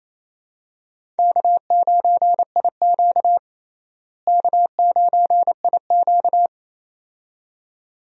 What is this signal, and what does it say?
Morse code, heard as a keyed tone, about 28 wpm, 700 Hz. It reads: K9IQ K9IQ
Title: USA Call Signs - 4 Characters - Repeated 2x - 28wpm